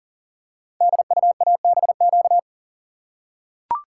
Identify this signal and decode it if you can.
Morse code, heard as a keyed tone, about 39 wpm, 690 Hz. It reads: DUABQ
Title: Code Groups - 5 Letters - 40wpm